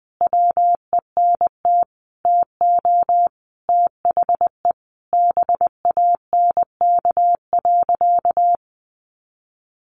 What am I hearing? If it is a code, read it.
Morse code, heard as a keyed tone, about 20 wpm, 700 Hz. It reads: WENT TO THE BANK.
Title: Sentences from Top 1300 Words 20wpm